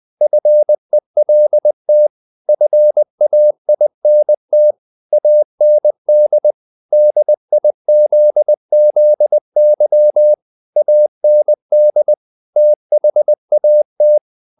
Morse code, heard as a keyed tone, about 20 wpm, 600 Hz. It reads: FELT FAINT AND DIZZY AND THAT
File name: the old man and the sea part 2 20wpm